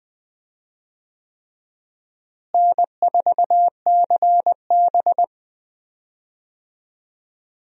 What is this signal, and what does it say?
Morse code, heard as a keyed tone, about 20 wpm, 700 Hz. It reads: N4CB
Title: USA Call Signs - 4 Characters - Repeated 3x - 20wpm